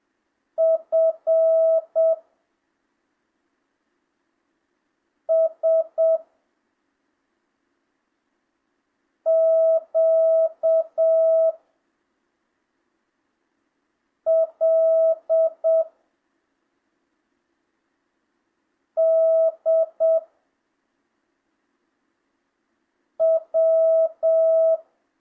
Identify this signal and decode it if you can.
Morse code, heard as a keyed tone, about 7 wpm, 650 Hz. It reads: F S Q L D W
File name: jh8cbh / 2025cw